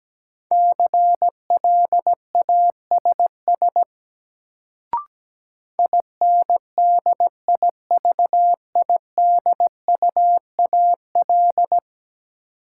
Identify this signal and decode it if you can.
Morse code, heard as a keyed tone, about 17 wpm, 700 Hz. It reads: CLASS INDIVIDUAL
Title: Sentences from Top 1400 Words - New Word Drill Alt - 17wpm